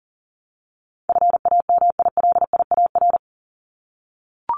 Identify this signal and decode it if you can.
Morse code, heard as a keyed tone, about 40 wpm, 700 Hz. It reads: FAMILIAR
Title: Top 751-1000 Words 40wpm